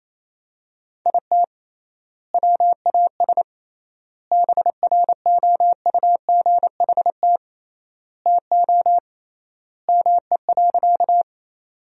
Morse code, heard as a keyed tone, about 28 wpm, 700 Hz. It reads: IT WAS BROUGHT TO ME.
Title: 3x Word-Spacing - Sentences from Top 500 Words - 28wpm